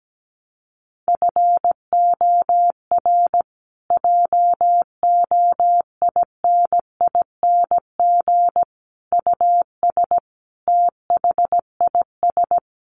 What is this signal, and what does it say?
Morse code, heard as a keyed tone, about 17 wpm, 700 Hz. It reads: FOR JOINING US THIS
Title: Sets of 5 Words Encore 17wpm